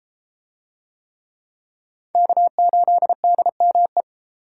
Morse code, heard as a keyed tone, about 30 wpm, 700 Hz. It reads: K8DME